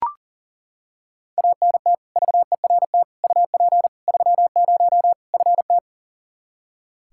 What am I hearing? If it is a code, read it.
Morse code, heard as a keyed tone, about 37 wpm, 700 Hz. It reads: ANT VERT UP 30 FT
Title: K1USN SST QSO Practice 40wpm